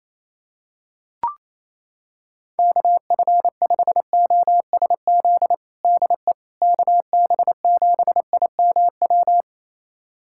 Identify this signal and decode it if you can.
Morse code, heard as a keyed tone, about 28 wpm, 700 Hz. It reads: KF5OSZ DE KB7IMW
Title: USA Callsign DE Callsign 28wpm